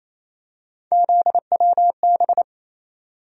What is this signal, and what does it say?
Morse code, heard as a keyed tone, about 28 wpm, 700 Hz. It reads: ZWB
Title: USA Call Sign Postfixes - 3 Letters 28wpm